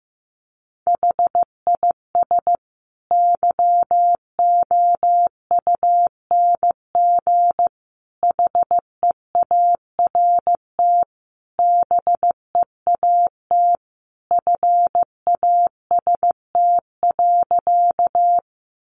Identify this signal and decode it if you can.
Morse code, heard as a keyed tone, about 15 wpm, 700 Hz. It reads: HIS YOUNG HEART BEAT FAST.
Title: Sentences from Top 1300 Words 15wpm